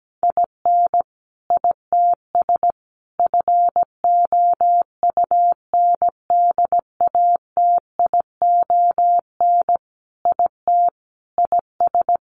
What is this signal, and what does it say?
Morse code, heard as a keyed tone, about 17 wpm, 700 Hz. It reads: IN ITS FOUNDATION IT IS
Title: Sentences from Top 1200 Words 17wpm